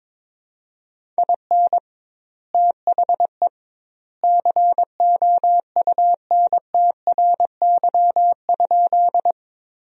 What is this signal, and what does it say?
Morse code, heard as a keyed tone, about 22 wpm, 700 Hz. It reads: IN THE COUNTRY?
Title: 2x Word-Spacing - Sentences from Top 300 Words - 22wpm